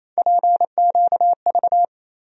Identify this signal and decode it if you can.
Morse code, heard as a keyed tone, about 28 wpm, 700 Hz. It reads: PQV
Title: USA Call Sign Postfixes - 3 Letters 28wpm